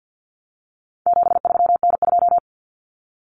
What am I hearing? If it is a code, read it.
Morse code, heard as a keyed tone, about 50 wpm, 700 Hz. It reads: 73N2
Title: Common Cut Numbers - 4-Digit 50wpm